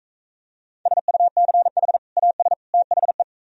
Morse code, heard as a keyed tone, about 43 wpm, 700 Hz. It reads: SUCH AS THE